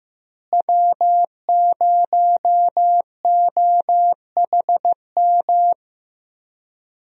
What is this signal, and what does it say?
Morse code, heard as a keyed tone, about 15 wpm, 700 Hz. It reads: W0OHM
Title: USA Call Signs - 5 Characters - Repeated 2x - 15wpm